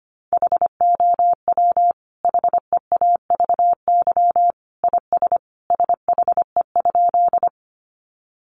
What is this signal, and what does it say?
Morse code, heard as a keyed tone, about 25 wpm, 700 Hz. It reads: HOW HEAVY IS SHE?
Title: Sentences from Top 1100 Words 25wpm